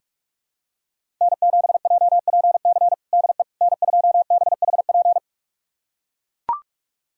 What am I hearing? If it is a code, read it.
Morse code, heard as a keyed tone, about 44 wpm, 700 Hz. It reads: N7JPC DE N2BHP
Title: USA Callsign DE Callsign - 5 Characters - 45wpm